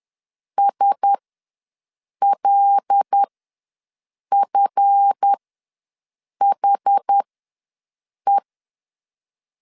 Morse code, heard as a keyed tone, about 10 wpm, spaced as for 6 wpm, 800 Hz. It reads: SLFHE
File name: LEZIONI / 7 GF9Z / 7 GF ws 10wpm 3918